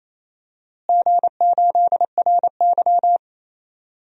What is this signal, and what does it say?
Morse code, heard as a keyed tone, about 28 wpm, 700 Hz. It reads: G8RY
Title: DXCC-100 Call Signs - 4 Characters - Repeated 2x - 28wpm